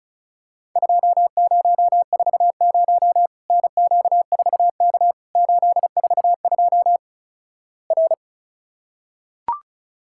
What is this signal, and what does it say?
Morse code, heard as a keyed tone, about 36 wpm, 700 Hz. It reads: 2040 NQ4K 842 R
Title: Worked All Europe DX Contest - QTC Exchange - 35wpm